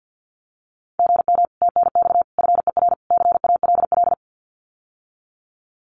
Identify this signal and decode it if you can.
Morse code, heard as a keyed tone, about 50 wpm, 700 Hz. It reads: GM TNX FER CALL
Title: Common QSO Phrases Encore 50wpm